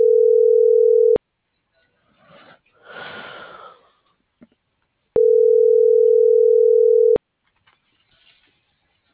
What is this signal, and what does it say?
Ringback tone (North America), 440 + 480 Hz.
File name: CENTRAL SAGOT / out danae.bsagot - .15278